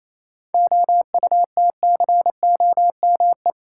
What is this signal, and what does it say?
Morse code, heard as a keyed tone, about 28 wpm, 700 Hz. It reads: OUTCOME
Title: Seven Letter Words - 28wpm